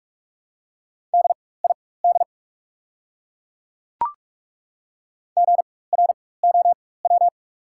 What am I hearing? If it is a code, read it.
Morse code, heard as a keyed tone, about 43 wpm, spaced as for 21 wpm, 690 Hz. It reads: DID GROW
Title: 4x Character Spacing - Top 300 Words - 45wpm